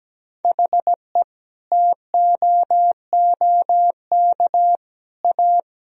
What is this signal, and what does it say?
Morse code, heard as a keyed tone, about 17 wpm, 700 Hz. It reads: HE TOOK A
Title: Sentences from Top 1000 Words - Review Part 1 - 17wpm